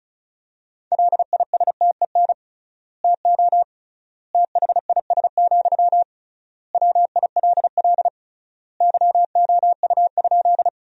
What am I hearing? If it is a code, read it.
Morse code, heard as a keyed tone, about 34 wpm, 700 Hz. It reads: LISTEN TO THIS, WILL YOU?
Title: 3x Word-Spacing - Sentences from Top 400 Words - 35wpm